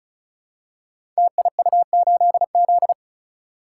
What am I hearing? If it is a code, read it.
Morse code, heard as a keyed tone, about 35 wpm, 700 Hz. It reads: TIU8Z
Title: Code Groups - 5 Letter:Numbers - 35wpm